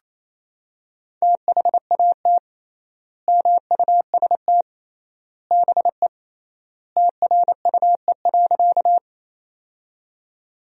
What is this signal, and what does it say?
Morse code, heard as a keyed tone, about 28 wpm, 700 Hz. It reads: THAT MUST BE TRUE.
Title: 3x Word-Spacing - Sentences from Top 400 Words - 28wpm